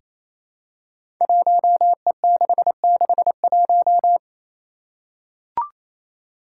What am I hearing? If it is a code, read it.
Morse code, heard as a keyed tone, about 28 wpm, 700 Hz. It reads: 1E661
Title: Five Challenging Letter-Number Combinations 28wpm